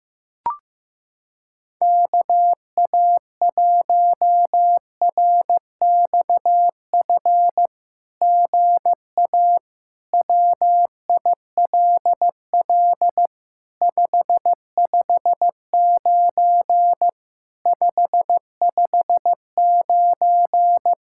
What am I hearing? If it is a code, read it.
Morse code, heard as a keyed tone, about 15 wpm, 700 Hz. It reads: KA1RXF GA WILL 559 559
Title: SOTA QSO Practice 15wpm